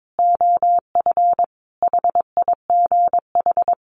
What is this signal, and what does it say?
Morse code, heard as a keyed tone, about 22 wpm, 700 Hz. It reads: OF HIGH
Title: Sentences from Top 1200 Words 22wpm